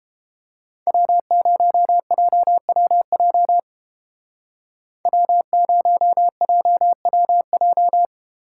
Morse code, heard as a keyed tone, about 30 wpm, 700 Hz. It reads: W0JWJ W0JWJ